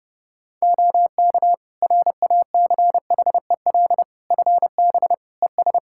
Morse code, heard as a keyed tone, about 30 wpm, 700 Hz. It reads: OK RACHEL FB ES
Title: Practice QSO - Exchange 2 - Info - 30wpm